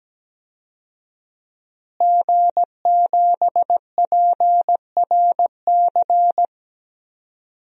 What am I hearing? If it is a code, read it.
Morse code, heard as a keyed tone, about 17 wpm, 700 Hz. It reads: G7PRC
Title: DXCC-100 Call Signs - 5 Characters - Repeated 2x - 17wpm